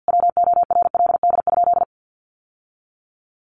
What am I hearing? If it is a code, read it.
Morse code, heard as a keyed tone, about 50 wpm, 700 Hz. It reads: WORLD?
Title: Sentences from Top 900 Words 50wpm